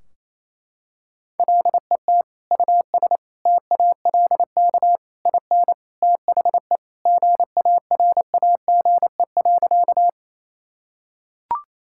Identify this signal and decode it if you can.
Morse code, heard as a keyed tone, about 28 wpm, 700 Hz. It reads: LET US TALK IN THE GARAGE.
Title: Sentences from Top 1600 Words 28wpm